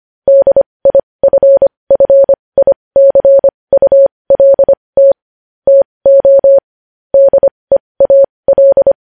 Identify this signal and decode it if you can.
Morse code, heard as a keyed tone, about 25 wpm, spaced as for 20 wpm, 550 Hz. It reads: DIFFICULT TO DEAL
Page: news-headlines-in-morse-code-at-20-wpm